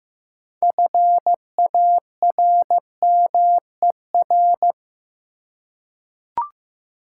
Morse code, heard as a keyed tone, about 15 wpm, 700 Hz. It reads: FARMER
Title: Six Letter Words - 15wpm